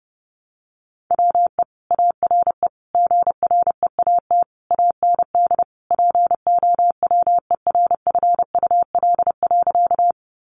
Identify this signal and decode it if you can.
Morse code, heard as a keyed tone, about 30 wpm, 700 Hz. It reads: WE ARE GREAT AND POWERFUL.
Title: Sentences from Top 1300 Words 30wpm